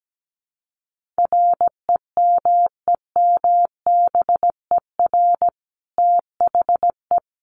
Morse code, heard as a keyed tone, about 17 wpm, 700 Hz. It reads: REMEMBER THE
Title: Sentences from Top 1400 Words 17wpm